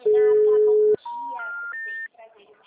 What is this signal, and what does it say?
Special information tone, with speech.